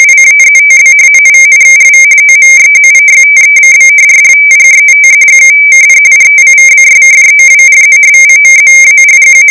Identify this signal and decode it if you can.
Two-tone teleprinter signal, about 170 Hz shift.